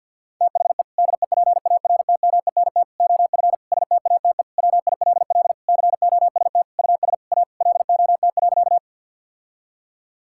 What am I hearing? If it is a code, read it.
Morse code, heard as a keyed tone, about 52 wpm, 700 Hz. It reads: THE DEPARTMENT OF STATE WILL COST US A LOT.
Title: Sentences from Top 1000 Words - Review Part 5 - 50wpm